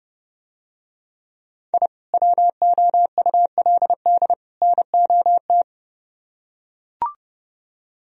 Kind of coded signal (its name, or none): Morse code